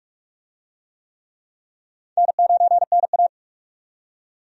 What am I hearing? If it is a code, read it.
Morse code, heard as a keyed tone, about 43 wpm, 700 Hz. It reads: N9NA
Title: USA Call Signs - 4 Characters - 45wpm